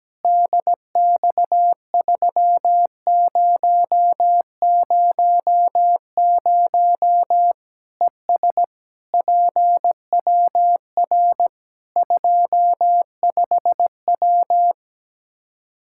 Morse code, heard as a keyed tone, about 17 wpm, 700 Hz. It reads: DX3000 ES PWR 25W
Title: K1USN SST QSO Practice 17wpm